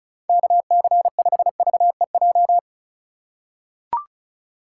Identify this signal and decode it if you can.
Morse code, heard as a keyed tone, about 36 wpm, 690 Hz. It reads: KC5VEJ
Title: USA Call Signs - 6 Characters - 6x Character Spacing - 35wpm